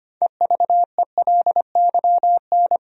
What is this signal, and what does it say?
Morse code, heard as a keyed tone, about 25 wpm, 700 Hz. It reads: EVELYN
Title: USA Common Names - 25wpm